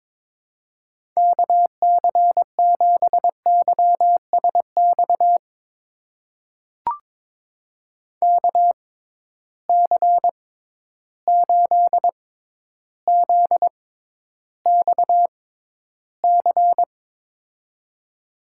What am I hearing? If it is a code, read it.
Morse code, heard as a keyed tone, about 22 wpm, spaced as for 11 wpm, 700 Hz. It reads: KC7YSX KC8ZXC
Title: USA Call Signs - 6 Characters - 6x Character Spacing - 22wpm